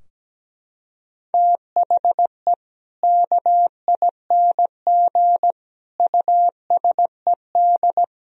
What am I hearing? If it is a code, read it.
Morse code, heard as a keyed tone, about 17 wpm, 700 Hz. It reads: THE KING USED
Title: Sentences from Top 1000 Words - Review Part 4 - 17wpm